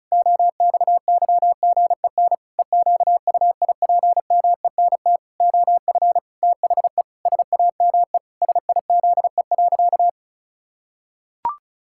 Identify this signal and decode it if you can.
Morse code, heard as a keyed tone, about 34 wpm, 700 Hz. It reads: OXYGEN EQUIPMENT OF THE SAME SIZE.
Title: Pangram Challenge v2 - 35wpm